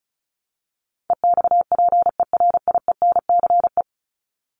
Morse code, heard as a keyed tone, about 34 wpm, 700 Hz. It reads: EXPERIENCE
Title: Top 501-750 Words 35wpm